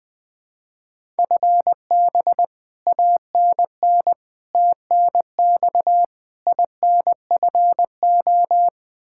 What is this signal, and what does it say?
Morse code, heard as a keyed tone, about 20 wpm, 700 Hz. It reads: FB ANN TNX INFO
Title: Practice QSO - Exchange 3 - More Info - 20wpm